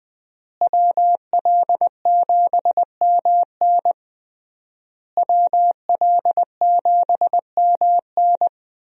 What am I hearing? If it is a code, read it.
Morse code, heard as a keyed tone, about 21 wpm, 700 Hz. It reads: WL7MN WL7MN
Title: USA Call Signs - 5 Characters - Repeated 3x - 20wpm